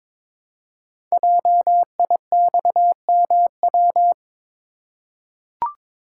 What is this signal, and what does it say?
Morse code, heard as a keyed tone, about 22 wpm, 700 Hz. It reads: JIXMW
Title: Code Groups - 5 Letters - 22wpm